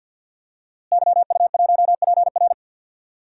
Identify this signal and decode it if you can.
Morse code, heard as a keyed tone, about 52 wpm, 700 Hz. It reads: YU1PR
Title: DXCC-100 Call Signs - 5 Characters - Repeated 2x - 50wpm